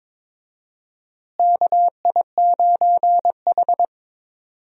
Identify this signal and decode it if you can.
Morse code, heard as a keyed tone, about 22 wpm, 700 Hz. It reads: KI9H
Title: USA Call Signs - 4 Characters - Repeated 2x - 22wpm